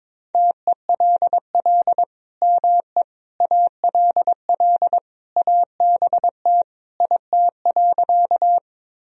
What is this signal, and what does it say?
Morse code, heard as a keyed tone, about 22 wpm, 700 Hz. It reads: TELL ME ALL ABT IT.
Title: Sentences from Top 200 Words and Abbreviations 22wpm